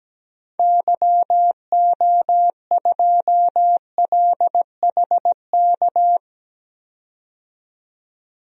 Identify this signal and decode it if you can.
Morse code, heard as a keyed tone, about 17 wpm, 700 Hz. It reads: YO2LHK
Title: DXCC-100 Call Signs - 6 Characters - 17wpm